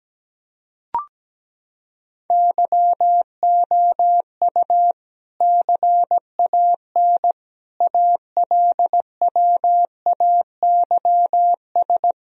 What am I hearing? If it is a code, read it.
Morse code, heard as a keyed tone, about 17 wpm, 700 Hz. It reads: YOU CAN ALWAYS
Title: Sentences from Top 1000 Words - Review Part 4 - 17wpm